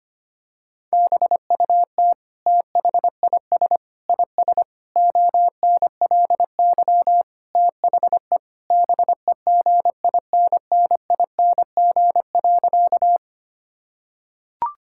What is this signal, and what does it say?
Morse code, heard as a keyed tone, about 25 wpm, 700 Hz. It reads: BUT THIS IS ONLY THE BEGINNING.
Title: Sentences from Top 1000 Words - Review Part 5 - 25wpm